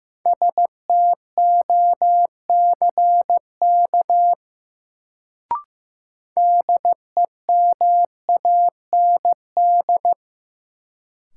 Morse code, heard as a keyed tone, about 15 wpm, 700 Hz. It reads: STOCK DEMAND
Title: Sentences from Top 800 Words - New Word Drill - 15wpm